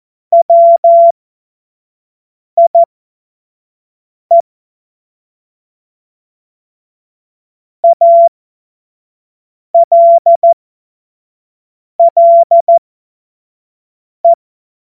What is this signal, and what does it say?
Morse code, heard as a keyed tone, about 14 wpm, spaced as for 5 wpm, 680 Hz. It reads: WIE ALLE
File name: Morsekurs / Witze / Witz 09 April 2024 25 F70